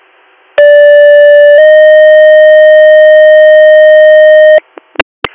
Two-tone page, heard as a one-second tone, then a three-second tone, about 580 then 620 Hz, with noise. It is heard from 0.6 to 4.6 s.